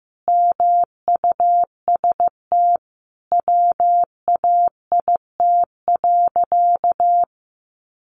Morse code, heard as a keyed tone, about 15 wpm, 700 Hz. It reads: MUST WAIT.
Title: Sentences from Top 500 Words 15wpm